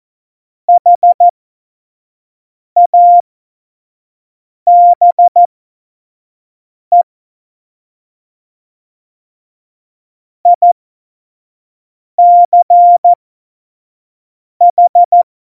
Morse code, heard as a keyed tone, about 14 wpm, spaced as for 5 wpm, 700 Hz. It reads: HABE ICH
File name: Morsekurs / Witze / Witz 22. Dezember 2020, 25 F70